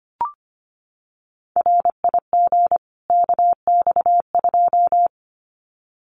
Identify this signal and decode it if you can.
Morse code, heard as a keyed tone, about 25 wpm, 700 Hz. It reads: RIG KX2
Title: Common QSO Phrases 25wpm